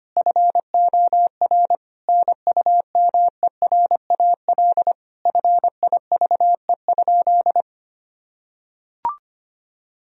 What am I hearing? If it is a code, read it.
Morse code, heard as a keyed tone, about 25 wpm, 700 Hz. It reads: FOR NUMERAL FIVE?
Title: Sentences from Top 1000 Words - Review Part 2 - 25wpm